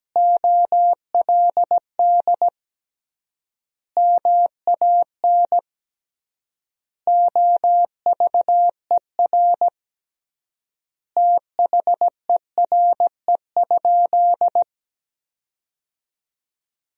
Morse code, heard as a keyed tone, about 17 wpm, 700 Hz. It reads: OLD MAN OVER THERE?
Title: 3x Word-Spacing - Sentences from Top 200 Words - 17wpm